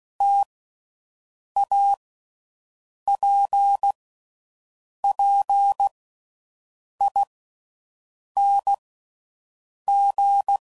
Morse code, heard as a keyed tone, about 16 wpm, 800 Hz. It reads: T A P P I N G
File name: Mixed / Variable Sp / T AOEIs